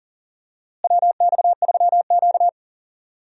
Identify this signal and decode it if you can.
Morse code, heard as a keyed tone, about 39 wpm, 700 Hz. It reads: WX3Q
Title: USA Call Signs - 4 Characters - Repeated 2x - 40wpm